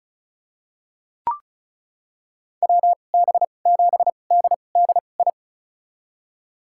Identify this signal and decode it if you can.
Morse code, heard as a keyed tone, about 36 wpm, spaced as for 25 wpm, 700 Hz. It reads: WB7DDI
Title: USA Call Signs - 2x Character Spacing - 35wpm